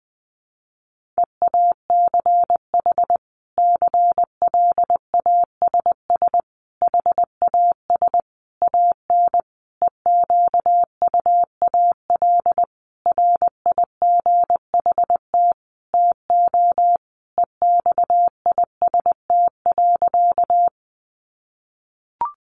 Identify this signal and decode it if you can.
Morse code, heard as a keyed tone, about 20 wpm, 700 Hz. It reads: EACH CLASS HAS AN EQUAL RIGHT TO EXIST.
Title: Sentences from Top 1400 Words 20wpm